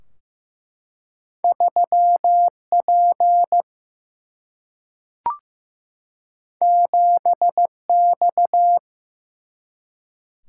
Morse code, heard as a keyed tone, about 15 wpm, 700 Hz. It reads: 3P 7X
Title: Single-Letter Single-Number 15wpm